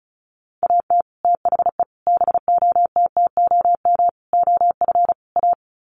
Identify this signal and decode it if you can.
Morse code, heard as a keyed tone, about 34 wpm, 700 Hz. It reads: AT THE BOTTOM OF A
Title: Sets of 5 Words Encore2 35wpm